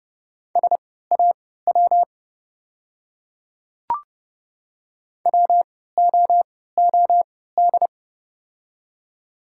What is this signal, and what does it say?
Morse code, heard as a keyed tone, about 32 wpm, spaced as for 17 wpm, 700 Hz. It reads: SAW WOOD
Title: 3x Character Spacing - Top 300 Words - 30wpm